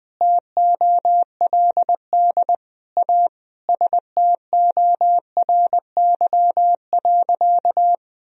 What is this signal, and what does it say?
Morse code, heard as a keyed tone, about 20 wpm, 700 Hz. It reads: TOLD A STORY.
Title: Sentences from Top 1000 Words - Review Part 1 - 20wpm